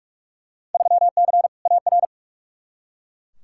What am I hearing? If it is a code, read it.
Morse code, heard as a keyed tone, about 47 wpm, 700 Hz. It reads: 3C AR
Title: Field Day Exchanges Encore - 45wpm